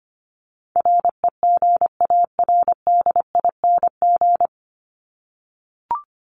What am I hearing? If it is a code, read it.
Morse code, heard as a keyed tone, about 25 wpm, 700 Hz. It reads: REGARDING
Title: ING Suffix - Top 100 Words with Base Words 25wpm